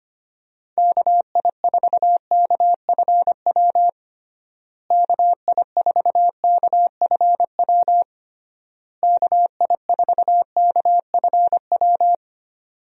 Morse code, heard as a keyed tone, about 25 wpm, 700 Hz. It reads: KI4KFW KI4KFW KI4KFW
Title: USA Call Signs - 6 Characters - Repeated 3x - 25wpm